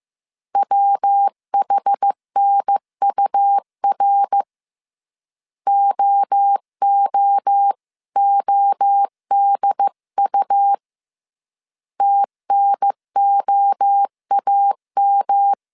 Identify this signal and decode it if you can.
Morse code, heard as a keyed tone, about 15 wpm, 800 Hz. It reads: WHNUR OOODU TNOAM